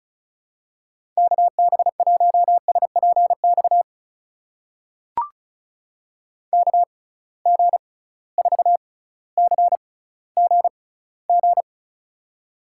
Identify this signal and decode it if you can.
Morse code, heard as a keyed tone, about 34 wpm, spaced as for 18 wpm, 700 Hz. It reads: KB1SPX KG4CGG
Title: USA Call Signs - 6 Characters - 6x Character Spacing - 35wpm